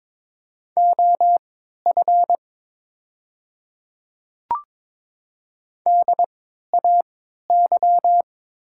Morse code, heard as a keyed tone, about 22 wpm, spaced as for 12 wpm, 700 Hz. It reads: OF DAY
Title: 3x Character Spacing - Top 100 Words - 22wpm